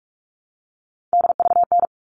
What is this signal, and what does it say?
Morse code, heard as a keyed tone, about 44 wpm, 700 Hz. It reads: DVN
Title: All Cut Numbers - 3-Digit 45wpm